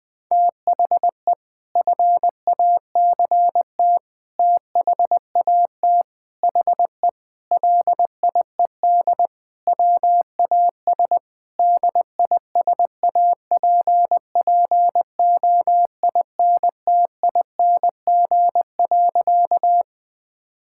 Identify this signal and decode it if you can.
Morse code, heard as a keyed tone, about 20 wpm, 700 Hz. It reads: THE FACT THAT HE LIED WAS DISAPPOINTING.
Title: Sentences with 4-Word Phrases - 20wpm